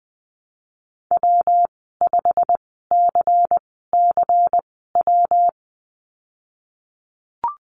Morse code, heard as a keyed tone, about 20 wpm, spaced as for 14 wpm, 700 Hz. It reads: W5CCW
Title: 154 Lesson 25g - Letter p - Call Signs 20wpm